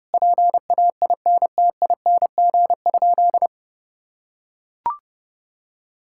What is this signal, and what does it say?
Morse code, heard as a keyed tone, about 32 wpm, 700 Hz. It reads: PAINTING?
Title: Sentences from Top 1000 Words - Review Part 6 - 30wpm